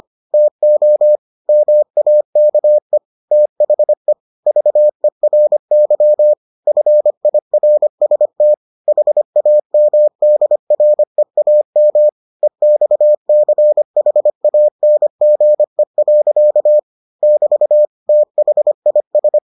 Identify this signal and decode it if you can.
Morse code, heard as a keyed tone, about 25 wpm, 600 Hz. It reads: TO MAKE THE VERY FIRST HAMDREAM EXCHANGE. <BT> THIS